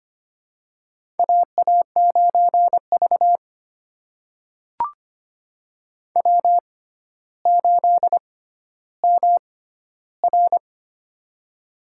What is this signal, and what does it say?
Morse code, heard as a keyed tone, about 25 wpm, spaced as for 13 wpm, 700 Hz. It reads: AA9V W8MR
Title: USA Call Signs - 4 Characters - 6x Character Spacing - 25wpm